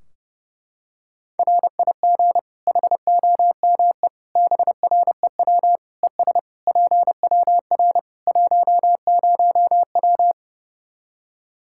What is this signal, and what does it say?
Morse code, heard as a keyed tone, about 28 wpm, 700 Hz. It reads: RIG HOME BREW ES PWR 10W
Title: Practice QSO - Exchange 2 - Info - 30wpm